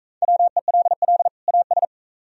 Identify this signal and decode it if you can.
Morse code, heard as a keyed tone, about 42 wpm, 700 Hz. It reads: WELL AS